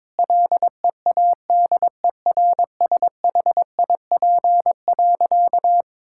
Morse code, heard as a keyed tone, about 21 wpm, 700 Hz. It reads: LEADERSHIP.
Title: Sentences from Top 1600 Words 22wpm